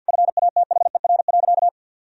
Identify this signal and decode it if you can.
Morse code, heard as a keyed tone, about 50 wpm, 700 Hz. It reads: FATHER.
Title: Sentences from Top 1000 Words - Review Part 2 - 50wpm